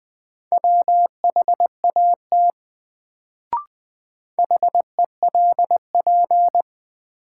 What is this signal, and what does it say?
Morse code, heard as a keyed tone, about 20 wpm, 700 Hz. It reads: WHAT HELP
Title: Sets of 2 Words - New Word Drill - 20wpm